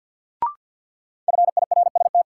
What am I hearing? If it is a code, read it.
Morse code, heard as a keyed tone, about 53 wpm, 700 Hz. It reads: FIRST
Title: Sets of 2 Words - New Word Drill - 50wpm